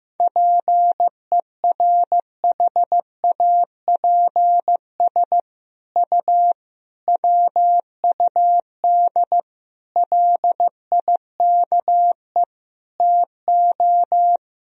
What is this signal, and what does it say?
Morse code, heard as a keyed tone, about 15 wpm, 700 Hz. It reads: PERHAPS U WUD LIKE TO
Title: Sentences from Top 500 Words and Abbreviations 15wpm